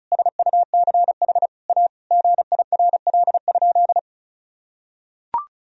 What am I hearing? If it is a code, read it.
Morse code, heard as a keyed tone, about 34 wpm, 700 Hz. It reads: SUCH A GIRL?
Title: Sentences from Top 1000 Words - Review Part 1 - 35wpm